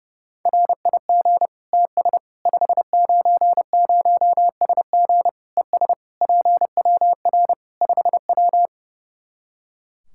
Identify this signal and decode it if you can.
Morse code, heard as a keyed tone, about 30 wpm, 700 Hz. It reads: RIG TS 590SG ES PWR 5W
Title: Practice QSO - Exchange 2 - Info - 30wpm